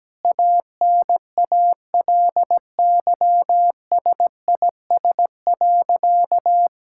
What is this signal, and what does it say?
Morse code, heard as a keyed tone, about 17 wpm, 700 Hz. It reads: ANALYSIS.
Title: Sentences from Top 1000 Words - Review Part 5 - 17wpm